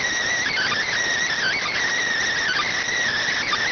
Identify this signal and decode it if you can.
Emergency Alert System (SAME) tones, heard as harsh buzzing bursts. It is heard from 0.3 to 2.9 s.